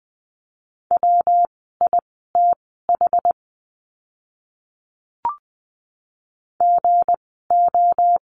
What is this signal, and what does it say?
Morse code, heard as a keyed tone, about 20 wpm, spaced as for 14 wpm, 700 Hz. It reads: WITH GO
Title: 199 Lesson 32f - Letter k - Words 20wpm